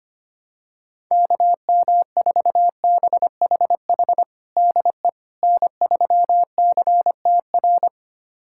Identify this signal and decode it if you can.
Morse code, heard as a keyed tone, about 25 wpm, 700 Hz. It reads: KM4BHH DE N3CTR
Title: USA Callsign DE Callsign 25wpm